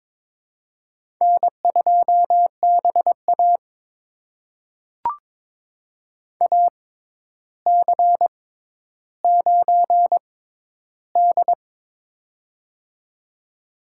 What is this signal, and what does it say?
Morse code, heard as a keyed tone, about 21 wpm, spaced as for 11 wpm, 700 Hz. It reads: N2BA AC9D
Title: USA Call Signs - 4 Characters - 6x Character Spacing - 22wpm